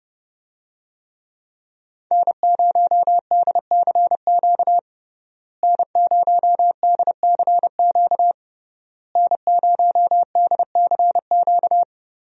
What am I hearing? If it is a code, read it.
Morse code, heard as a keyed tone, about 30 wpm, 700 Hz. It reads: N0DCQ N0DCQ N0DCQ
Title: USA Call Signs - 5 Characters - Repeated 3x - 30wpm